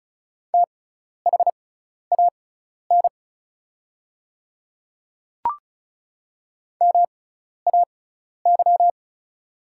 Morse code, heard as a keyed tone, about 34 wpm, spaced as for 12 wpm, 700 Hz. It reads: THAN MAY
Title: 6x Character Spacing - Top 200 Words - 35wpm